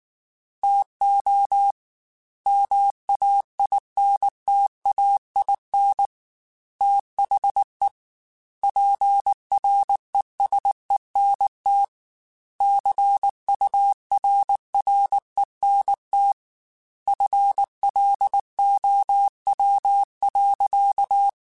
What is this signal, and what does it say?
Morse code, heard as a keyed tone, about 19 wpm, 800 Hz. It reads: TO MAINTAIN THE PRESENT CURRENT FLOW.